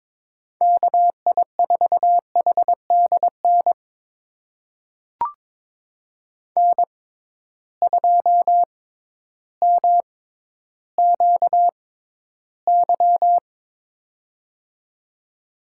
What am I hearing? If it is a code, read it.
Morse code, heard as a keyed tone, about 22 wpm, spaced as for 12 wpm, 700 Hz. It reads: KI4HDN N2MQY
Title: USA Call Signs - 6x Character Spacing - 22wpm